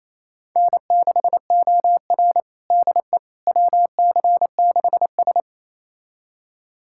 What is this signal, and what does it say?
Morse code, heard as a keyed tone, about 28 wpm, 700 Hz. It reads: N6OR DE WC6S
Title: USA Callsign DE Callsign - 4 Characters - 28wpm